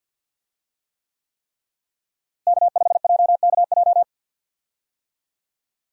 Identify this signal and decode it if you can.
Morse code, heard as a keyed tone, about 52 wpm, 700 Hz. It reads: K5JKJ
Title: USA Call Signs - 5 Characters - Repeated 3x - 50wpm